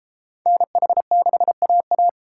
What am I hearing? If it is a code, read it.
Morse code, heard as a keyed tone, about 31 wpm, 700 Hz. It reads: NH6AA